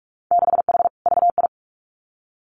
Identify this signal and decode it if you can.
Morse code, heard as a keyed tone, about 46 wpm, 700 Hz. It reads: 6H VI
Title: Winter Field Day Exchanges Encore 45wpm